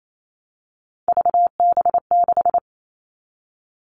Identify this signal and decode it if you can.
Morse code, heard as a keyed tone, about 28 wpm, 700 Hz. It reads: VB6
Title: All Cut Numbers - 3-Digit 28wpm